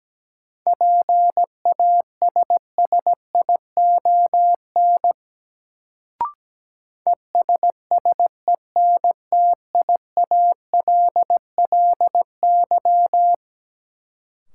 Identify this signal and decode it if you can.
Morse code, heard as a keyed tone, about 17 wpm, 700 Hz. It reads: PASSION ESSENTIALLY
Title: Sentences from Top 1300 Words - New Word Drill - 17wpm